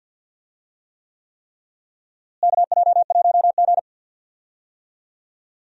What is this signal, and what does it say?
Morse code, heard as a keyed tone, about 52 wpm, 700 Hz. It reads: KJ1G
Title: USA Call Signs - 4 Characters - Repeated 3x - 50wpm